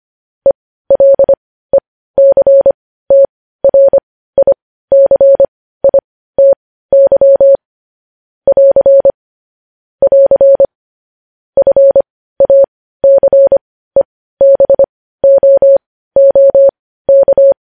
Morse code, heard as a keyed tone, about 25 wpm, spaced as for 15 wpm, 550 Hz. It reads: ELECTRICITY <AR> <AR> FACEBOOK
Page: News headlines in Morse code at 15 wpm